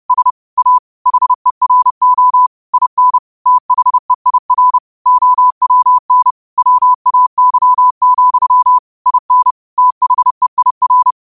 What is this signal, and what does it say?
Morse code, heard as a keyed tone, about 31 wpm, 1000 Hz. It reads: S A HERO IN THEIR OWN WAY, IN THEIR